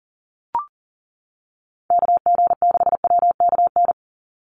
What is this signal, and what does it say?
Morse code, heard as a keyed tone, about 40 wpm, 700 Hz. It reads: KG6WKN
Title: USA Call Signs 40wpm